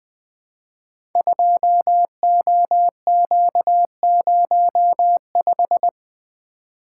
Morse code, heard as a keyed tone, about 20 wpm, 700 Hz. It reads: 2OQ05
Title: Code Groups - 5 Letter:Numbers - 20wpm